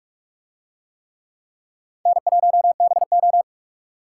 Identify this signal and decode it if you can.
Morse code, heard as a keyed tone, about 40 wpm, 700 Hz. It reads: N1BO